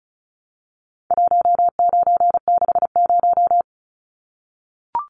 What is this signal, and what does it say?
Morse code, heard as a keyed tone, about 34 wpm, 700 Hz. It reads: 1960
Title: Numbers - 4-Digit 35wpm